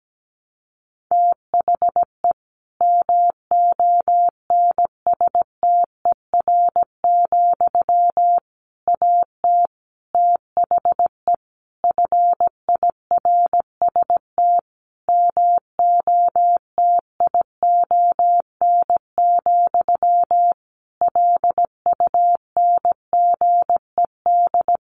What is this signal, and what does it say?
Morse code, heard as a keyed tone, about 17 wpm, 700 Hz. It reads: THE MONSTER, AT THE FIRST MOTION, LUNGED
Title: A Sound Of Thunder 17wpm